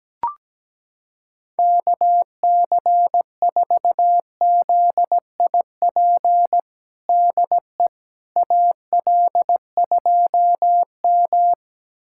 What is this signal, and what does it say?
Morse code, heard as a keyed tone, about 17 wpm, 700 Hz. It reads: KC4ZIP DE AL2M
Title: USA Callsign DE Callsign 17wpm